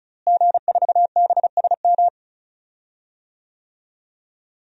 Morse code, heard as a keyed tone, about 36 wpm, 700 Hz. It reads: G4BSM
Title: DXCC-100 Call Signs - 5 Characters - Repeated 3x - 35wpm